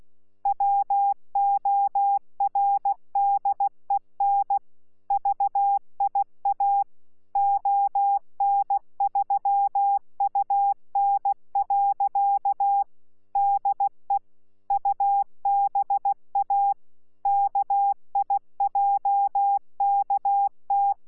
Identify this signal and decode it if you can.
Morse code, heard as a keyed tone, about 16 wpm, 800 Hz. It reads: WORDEN VIA ON3UN. DE UBA KIJKT
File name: Z PI4VRZ / PI4VRZ 14-jun-2025